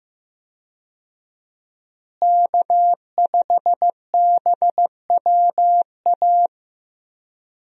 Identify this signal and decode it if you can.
Morse code, heard as a keyed tone, about 15 wpm, 700 Hz. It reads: K5BWA
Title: USA Call Signs - 5 Characters - Repeated 3x - 15wpm